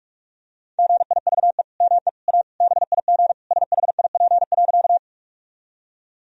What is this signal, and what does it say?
Morse code, heard as a keyed tone, about 46 wpm, 700 Hz. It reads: GIVE ME A BIG SHIP.
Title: Sentences from Top 1000 Words - Review Part 2 - 45wpm